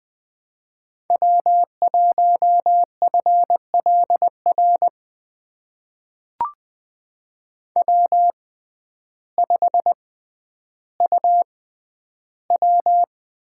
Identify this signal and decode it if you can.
Morse code, heard as a keyed tone, about 20 wpm, spaced as for 11 wpm, 700 Hz. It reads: W1FLR W5UW
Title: USA Call Signs - 5 Characters - 6x Character Spacing - 20wpm